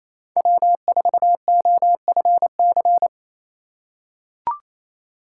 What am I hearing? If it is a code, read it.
Morse code, heard as a keyed tone, about 28 wpm, 700 Hz. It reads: W4OFC
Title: USA Call Signs - 6x Character Spacing - 28wpm